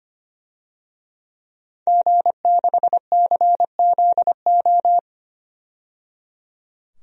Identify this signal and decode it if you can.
Morse code, heard as a keyed tone, about 25 wpm, 700 Hz. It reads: G6CZO
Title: DXCC-100 Call Signs - 5 Characters - 25wpm